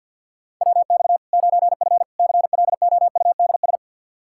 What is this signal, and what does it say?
Morse code, heard as a keyed tone, about 53 wpm, 700 Hz. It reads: WX 9F CLOUDS
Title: K1USN SST QSO Practice 50wpm